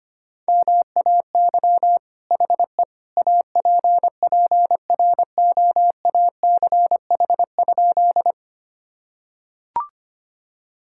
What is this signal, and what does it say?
Morse code, heard as a keyed tone, about 25 wpm, 700 Hz. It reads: MAY HE APPROACH?
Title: Sentences from Top 1000 Words - Review Part 6 - 25wpm